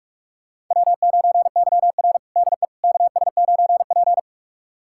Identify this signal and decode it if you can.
Morse code, heard as a keyed tone, about 42 wpm, 690 Hz. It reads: W9YR DE KS9P
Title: USA Callsign DE Callsign - 4 Characters - 45wpm